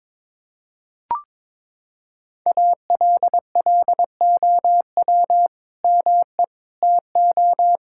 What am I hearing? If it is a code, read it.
Morse code, heard as a keyed tone, about 22 wpm, 700 Hz. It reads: ALLOW ME TO
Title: Sentences from Top 1600 Words 22wpm